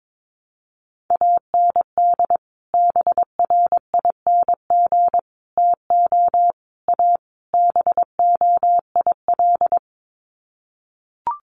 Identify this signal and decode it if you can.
Morse code, heard as a keyed tone, about 22 wpm, 700 Hz. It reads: AND BRING TO A BOIL
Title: Sets of 5 Words Encore 22wpm